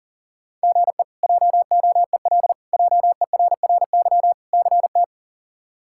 Morse code, heard as a keyed tone, about 40 wpm, 700 Hz. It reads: GE JOEL JERRY CT
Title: K1USN SST QSO Practice - 40wpm